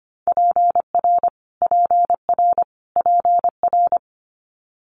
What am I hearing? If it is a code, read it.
Morse code, heard as a keyed tone, about 25 wpm, 700 Hz. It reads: PR PR PR
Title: Multi-letter Phonemes Repeated 3x 25wpm